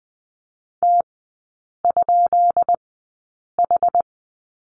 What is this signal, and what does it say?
Morse code, heard as a keyed tone, about 20 wpm, spaced as for 8 wpm, 690 Hz. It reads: T?H
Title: 145 Lesson 24e - Letter y - Three Characters 20wpm